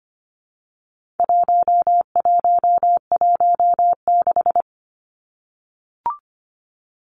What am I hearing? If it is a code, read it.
Morse code, heard as a keyed tone, about 25 wpm, 700 Hz. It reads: 1116
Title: Numbers - 4-Digit 25wpm